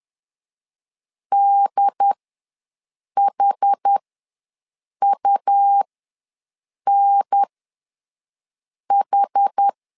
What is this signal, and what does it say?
Morse code, heard as a keyed tone, about 10 wpm, spaced as for 6 wpm, 800 Hz. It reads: DHUNH
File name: LEZIONI / 4 Q ws 10wpm 2306